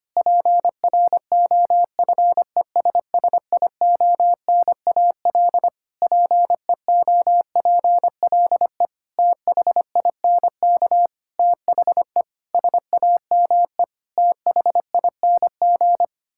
Morse code, heard as a keyed tone, about 25 wpm, 700 Hz. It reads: PROFESSIONAL PEOPLE THINK THE SAME THING
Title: Sentences from Top 1000 Words - Review Part 5 - 25wpm